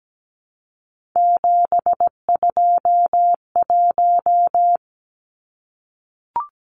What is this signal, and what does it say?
Morse code, heard as a keyed tone, about 17 wpm, 700 Hz. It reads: 721
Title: Numbers - 3-Digit 17wpm